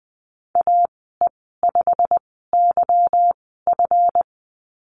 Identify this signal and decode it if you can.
Morse code, heard as a keyed tone, about 20 wpm, spaced as for 15 wpm, 700 Hz. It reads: AE5YF
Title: 173 Lesson 28g - Number 9 - Call Signs 20wpm